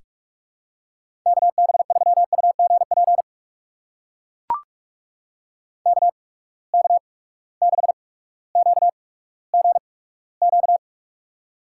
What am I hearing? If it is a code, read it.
Morse code, heard as a keyed tone, about 42 wpm, spaced as for 19 wpm, 700 Hz. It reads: KB3UGP KK6QGQ
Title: USA Call Signs - 6 Characters - 8x Character Spacing - 45wpm